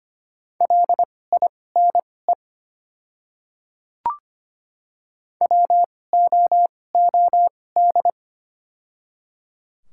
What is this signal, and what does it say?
Morse code, heard as a keyed tone, about 25 wpm, spaced as for 18 wpm, 700 Hz. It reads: LINE WOOD
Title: 2x Character Spacing - Top 300 Words - 25wpm